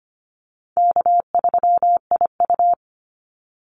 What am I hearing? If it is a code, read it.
Morse code, heard as a keyed tone, about 25 wpm, 700 Hz. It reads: K3IU
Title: Call Signs Repeated 2x - 25wpm